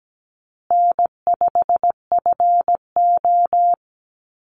Morse code, heard as a keyed tone, about 17 wpm, 700 Hz. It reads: N5FO
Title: Call Signs Repeated 2x - 17wpm